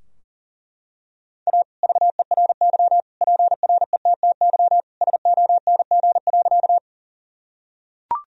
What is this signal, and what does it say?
Morse code, heard as a keyed tone, about 37 wpm, 700 Hz. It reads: A VERY PRETTY SONG.
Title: Sentences from Top 1000 Words - Review Part 2 - 40wpm